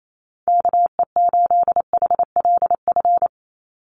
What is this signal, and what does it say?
Morse code, heard as a keyed tone, about 28 wpm, 700 Hz. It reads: KE8HLF
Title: USA Call Signs 28wpm